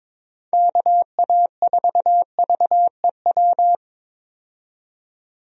Morse code, heard as a keyed tone, about 22 wpm, 700 Hz. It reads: KA4VEW
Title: USA Call Signs - 6 Characters - 22wpm